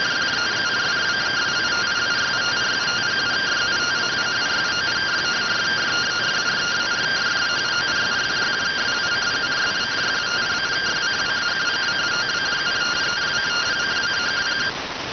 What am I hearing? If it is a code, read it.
Two-tone teleprinter signal, about 170 Hz shift.